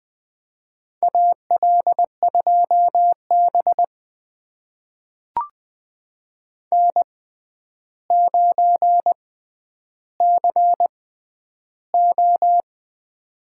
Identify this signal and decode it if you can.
Morse code, heard as a keyed tone, about 20 wpm, spaced as for 10 wpm, 700 Hz. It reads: AL2B N9CO
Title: USA Call Signs - 4 Characters - 6x Character Spacing - 20wpm